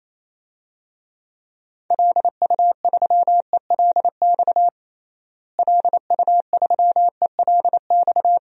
Morse code, heard as a keyed tone, about 28 wpm, 700 Hz. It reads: LU3ELX LU3ELX
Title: DXCC-100 Call Signs - 6 Characters - Repeated 3x - 28wpm